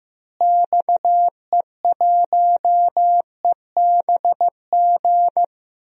Morse code, heard as a keyed tone, about 16 wpm, 700 Hz. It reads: XE1EBG
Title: DXCC-100 Call Signs - 6 Characters - Repeated 2x - 15wpm